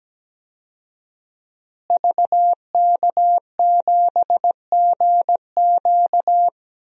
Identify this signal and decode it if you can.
Morse code, heard as a keyed tone, about 17 wpm, 700 Hz. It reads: VK7GQ
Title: DXCC-100 Call Signs - 5 Characters - Repeated 2x - 17wpm